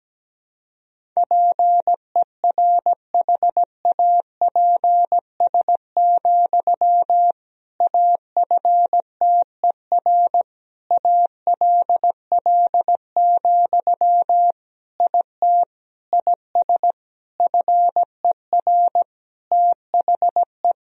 Morse code, heard as a keyed tone, about 17 wpm, 700 Hz. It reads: PERHAPS, AFTER ALL, IT IS FER THE
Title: Sentences from Top 500 Words and Abbreviations 17wpm